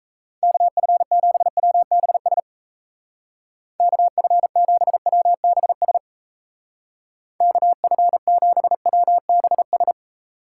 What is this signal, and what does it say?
Morse code, heard as a keyed tone, about 34 wpm, 700 Hz. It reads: KF7WBS KF7WBS KF7WBS
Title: Speed Racing - USA 2x3 Call Signs 28wpm